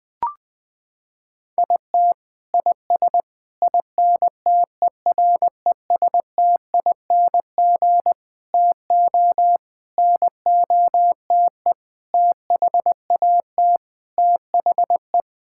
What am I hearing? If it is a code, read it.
Morse code, heard as a keyed tone, about 20 wpm, 700 Hz. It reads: IT IS INTERESTING TO NOTE THAT THE
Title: Sentences with 6-Word Phrases - 20wpm